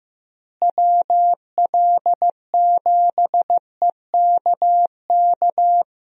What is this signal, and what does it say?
Morse code, heard as a keyed tone, about 15 wpm, 700 Hz. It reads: WL7EKK
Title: USA Call Signs - 6 Characters - Repeated 3x - 15wpm